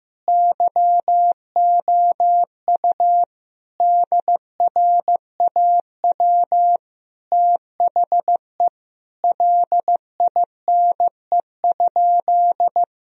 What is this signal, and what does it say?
Morse code, heard as a keyed tone, about 15 wpm, 700 Hz. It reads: YOU DRAW THE LINE?
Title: Sentences from Top 1000 Words - Review Part 1 - 15wpm